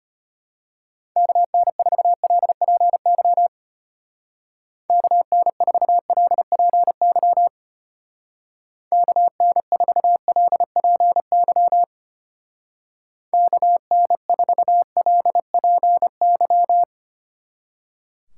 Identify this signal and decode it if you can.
Morse code, heard as a keyed tone, about 31 wpm, 700 Hz. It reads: KN4LPY KN4LPY KN4LPY KN4LPY
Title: Speed Racing - USA Call Signs 25wpm